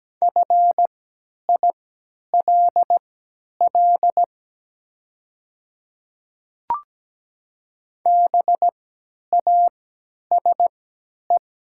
Morse code, heard as a keyed tone, about 17 wpm, spaced as for 10 wpm, 700 Hz. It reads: FILL BASE
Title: 3x Character Spacing - Top 500 Words - 17wpm